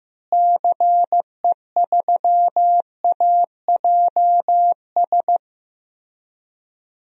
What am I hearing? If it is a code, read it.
Morse code, heard as a keyed tone, about 15 wpm, 700 Hz. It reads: CE3AJS
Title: DXCC-100 Call Signs - 6 Characters - Repeated 3x - 15wpm